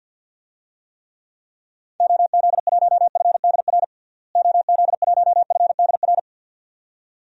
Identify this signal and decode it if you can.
Morse code, heard as a keyed tone, about 53 wpm, 700 Hz. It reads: OZ1FDR OZ1FDR
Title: DXCC-100 Call Signs - 6 Characters - Repeated 2x - 50wpm